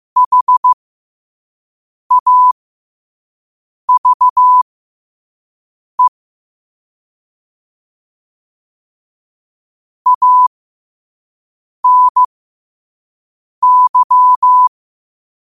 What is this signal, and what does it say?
Morse code, heard as a keyed tone, about 15 wpm, spaced as for 5 wpm, 1000 Hz. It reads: HAVE ANY
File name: qotdfiles / 5wpm